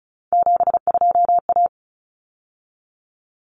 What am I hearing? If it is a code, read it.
Morse code, heard as a keyed tone, about 34 wpm, 700 Hz. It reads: 72A
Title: Common Cut Numbers - 3-Digit 35wpm